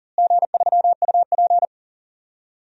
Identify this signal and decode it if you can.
Morse code, heard as a keyed tone, about 40 wpm, 700 Hz. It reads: G3UP
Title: DXCC-100 Call Signs - 4 Characters - Repeated 2x - 40wpm